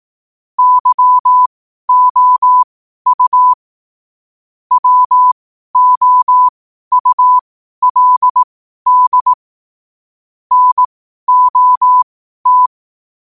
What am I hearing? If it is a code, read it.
Morse code, heard as a keyed tone, about 18 wpm, spaced as for 12 wpm, 1000 Hz. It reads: YOU WOULD NOT
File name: qotdfiles / 12wpm